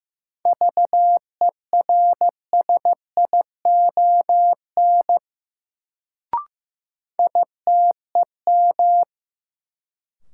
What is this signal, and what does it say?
Morse code, heard as a keyed tone, about 15 wpm, 700 Hz. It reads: VERSION ITEM
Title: Sentences from Top 1400 Words - New Word Drill Alt - 15wpm